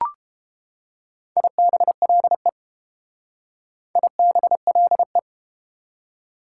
Morse code, heard as a keyed tone, about 30 wpm, 690 Hz. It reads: IBLE IBLE